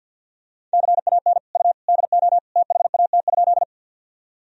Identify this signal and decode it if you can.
Morse code, heard as a keyed tone, about 52 wpm, 700 Hz. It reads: CAN U DO THAT?
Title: Sentences from Top 100 Words and Abbreviations 50wpm